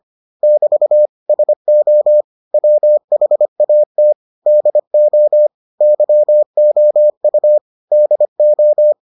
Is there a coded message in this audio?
Yes: Morse code from 0.4 to 8.9 s, heard as a keyed tone, about 25 wpm, 600 Hz. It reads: <BT> SO WHAT DO YOU DO